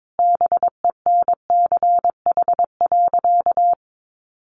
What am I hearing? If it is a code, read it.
Morse code, heard as a keyed tone, about 22 wpm, 700 Hz. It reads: BENCH.
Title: Sentences from Top 1400 Words 22wpm